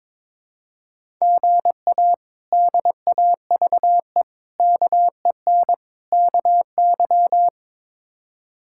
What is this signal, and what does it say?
Morse code, heard as a keyed tone, about 22 wpm, 700 Hz. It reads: GA DAVE KEN KY
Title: K1USN SST QSO Practice - 22wpm